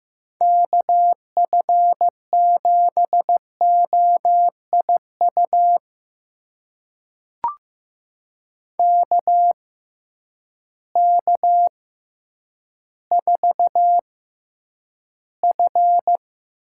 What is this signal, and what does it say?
Morse code, heard as a keyed tone, about 15 wpm, spaced as for 9 wpm, 700 Hz. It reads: KF7OIU KK4F
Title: USA Call Signs - 6 Characters - 6x Character Spacing - 15wpm